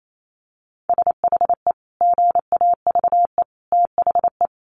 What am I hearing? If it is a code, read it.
Morse code, heard as a keyed tone, about 28 wpm, 690 Hz. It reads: SHE GAVE THE
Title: Sentences from Top 900 Words 28wpm